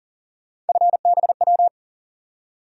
Morse code, heard as a keyed tone, about 40 wpm, 700 Hz. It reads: FBW
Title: USA Call Sign Postfixes - 3 Letters 40wpm